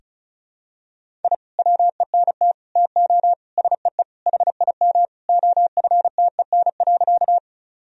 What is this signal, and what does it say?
Morse code, heard as a keyed tone, about 36 wpm, 700 Hz. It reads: I WENT TO SEE HIM OFTEN.
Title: Sentences from Top 1000 Words - Review Part 1 - 35wpm